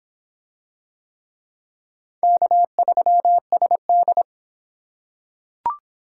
Morse code, heard as a keyed tone, about 24 wpm, 700 Hz. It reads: K3SD